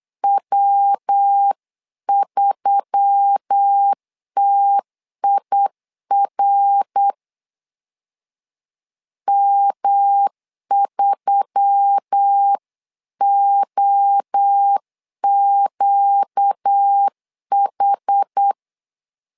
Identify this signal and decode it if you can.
Morse code, heard as a keyed tone, about 8 wpm, 800 Hz. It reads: W3TIR M3OQH
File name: LEZIONI / 4 Q1S3 8wpm 3037